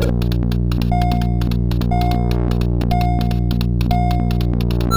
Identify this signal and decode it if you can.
There is no code.